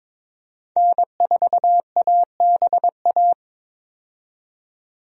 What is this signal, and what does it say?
Morse code, heard as a keyed tone, about 22 wpm, 700 Hz. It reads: N4ABA
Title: USA Call Signs - 5 Characters - 22wpm